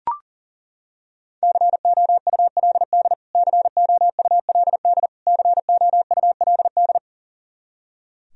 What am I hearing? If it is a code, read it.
Morse code, heard as a keyed tone, about 40 wpm, 700 Hz. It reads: COULD COULD COULD
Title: Top 100 Words Repeated 40wpm